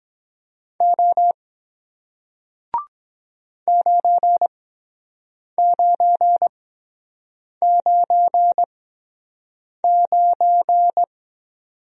Morse code, heard as a keyed tone, about 23 wpm, spaced as for 7 wpm, 700 Hz. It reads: O 9999